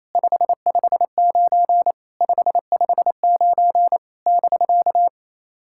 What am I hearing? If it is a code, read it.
Morse code, heard as a keyed tone, about 27 wpm, 690 Hz. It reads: 559 559 <BK>
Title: SOTA QSO Practice 28wpm